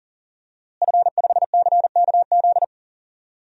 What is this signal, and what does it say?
Morse code, heard as a keyed tone, about 40 wpm, 700 Hz. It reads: F5CKZ
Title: DXCC-100 Call Signs - 5 Characters - Repeated 2x - 40wpm